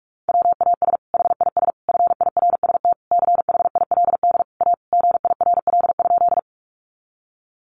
Morse code, heard as a keyed tone, about 44 wpm, 700 Hz. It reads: WAS HIS FIRST CHILD A GIRL?
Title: Sentences from Top 800 Words 45wpm